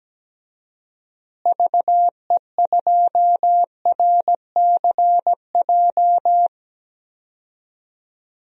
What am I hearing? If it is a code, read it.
Morse code, heard as a keyed tone, about 17 wpm, 700 Hz. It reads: VE2RCJ
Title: DXCC-100 Call Signs - 6 Characters - Repeated 2x - 17wpm